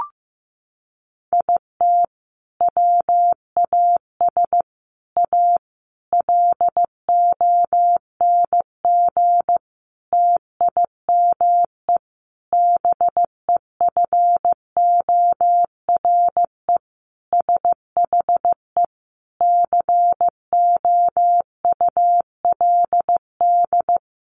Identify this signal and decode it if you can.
Morse code, heard as a keyed tone, about 15 wpm, 700 Hz. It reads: IT WAS A LONG TIME BEFORE SHE COULD
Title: Sentences from Top 500 Words 15wpm